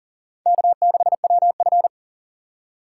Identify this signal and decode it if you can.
Morse code, heard as a keyed tone, about 51 wpm, 700 Hz. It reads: K6WF